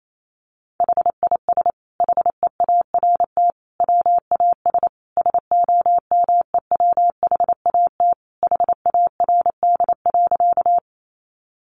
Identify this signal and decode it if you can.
Morse code, heard as a keyed tone, about 28 wpm, 700 Hz. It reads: HIS HEART WAS SOMEWHAT HARD.
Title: Sentences from Top 1300 Words 28wpm